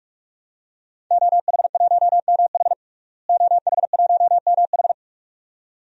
Morse code, heard as a keyed tone, about 46 wpm, 700 Hz. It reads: OH1MH OH1MH
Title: DXCC-100 Call Signs - 5 Characters - Repeated 2x - 45wpm